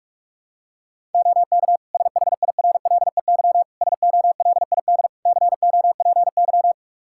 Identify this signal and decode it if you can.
Morse code, heard as a keyed tone, about 46 wpm, 700 Hz. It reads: OK SHIRLEY SOLID COPY
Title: Practice QSO - Exchange 3 - More Info - 45wpm